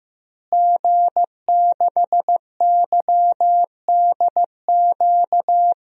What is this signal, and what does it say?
Morse code, heard as a keyed tone, about 15 wpm, 700 Hz. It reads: G6YDQ
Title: DXCC-100 Call Signs - 5 Characters - Repeated 2x - 15wpm